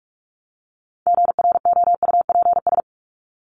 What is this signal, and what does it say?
Morse code, heard as a keyed tone, about 44 wpm, 700 Hz. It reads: GROUPS
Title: ES:S Suffix - Top 100 Words with Base Words 45wpm